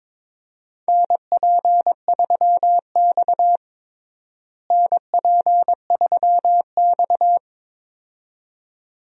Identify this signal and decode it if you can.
Morse code, heard as a keyed tone, about 21 wpm, 700 Hz. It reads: NP3X NP3X
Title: USA Call Signs - 4 Characters - Repeated 2x - 22wpm